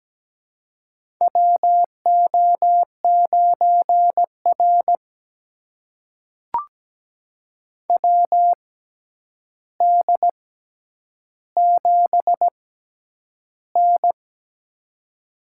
Morse code, heard as a keyed tone, about 17 wpm, spaced as for 9 wpm, 700 Hz. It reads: WO9R WD7N
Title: USA Call Signs - 4 Characters - 6x Character Spacing - 17wpm